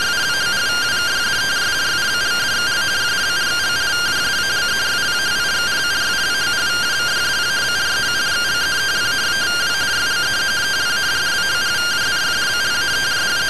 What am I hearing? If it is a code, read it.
Two-tone teleprinter signal, about 170 Hz shift.